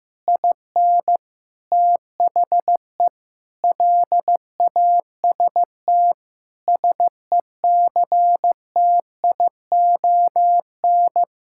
Morse code, heard as a keyed tone, about 15 wpm, 700 Hz. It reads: IN THE LAST SECTION
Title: Sentences from Top 1000 Words - Review Part 5 - 15wpm